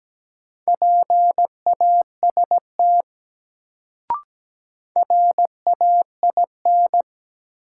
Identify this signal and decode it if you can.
Morse code, heard as a keyed tone, about 17 wpm, 700 Hz. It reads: PAST RAIN
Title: Sentences from Top 600 Words - New Word Drill - 17wpm